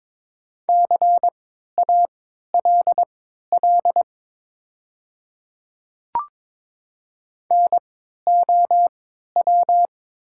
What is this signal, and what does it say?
Morse code, heard as a keyed tone, about 22 wpm, spaced as for 12 wpm, 700 Hz. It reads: CALL NOW
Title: 3x Character Spacing - Top 100 Words - 22wpm